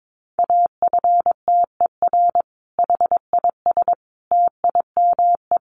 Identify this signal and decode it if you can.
Morse code, heard as a keyed tone, about 22 wpm, 700 Hz. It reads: AFTER HIS TIME
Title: Sentences from Top 1400 Words 22wpm